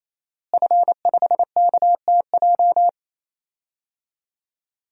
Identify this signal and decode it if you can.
Morse code, heard as a keyed tone, about 28 wpm, 700 Hz. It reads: F5KTJ
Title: DXCC-100 Call Signs - 5 Characters - Repeated 3x - 28wpm